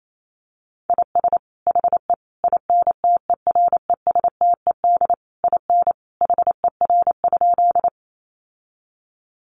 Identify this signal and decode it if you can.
Morse code, heard as a keyed tone, about 28 wpm, 700 Hz. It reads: IS HE INTERESTED IN HER?
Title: Sentences from Top 900 Words 28wpm